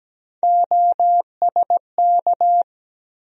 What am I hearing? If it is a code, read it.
Morse code, heard as a keyed tone, about 17 wpm, 700 Hz. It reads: OSK
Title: USA Call Sign Postfixes - 3 Letters 17wpm